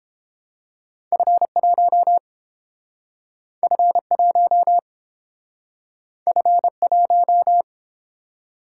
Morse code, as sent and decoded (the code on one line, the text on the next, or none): ..-. .---- / ..-. .---- / ..-. .----
F1 F1 F1